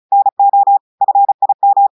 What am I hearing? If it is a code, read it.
Morse code, heard as a keyed tone, about 35 wpm, 800 Hz. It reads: NO FIM
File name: audios / portugais / portugais texte en clair 2 vitesse 35 mots minutes